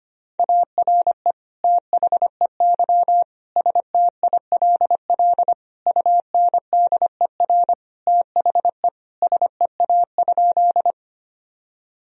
Morse code, heard as a keyed tone, about 25 wpm, 700 Hz. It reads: ARE THEY STILL UNDER THE SEA?
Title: Sentences from Top 1000 Words - Review Part 1 - 25wpm